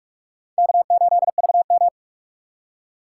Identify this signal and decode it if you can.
Morse code, heard as a keyed tone, about 47 wpm, 700 Hz. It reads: K8VM
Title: USA Call Signs - 4 Characters - 2x Character Spacing - 45wpm